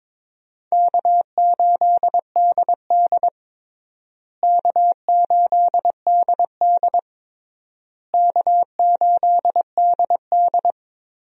Morse code, heard as a keyed tone, about 22 wpm, 700 Hz. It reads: K8DD K8DD K8DD
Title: USA Call Signs - 4 Characters - Repeated 3x - 22wpm